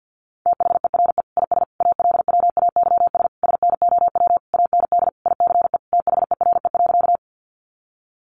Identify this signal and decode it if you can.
Morse code, heard as a keyed tone, about 50 wpm, 700 Hz. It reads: THERE IS ALWAYS SNOW AND ICE THERE.
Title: Sentences from Top 1000 Words 50wpm